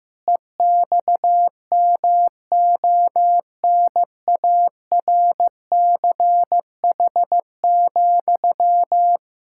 Morse code, heard as a keyed tone, about 15 wpm, 700 Hz. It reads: EXMONARCH,
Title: Pangram Challenge v2 - 15wpm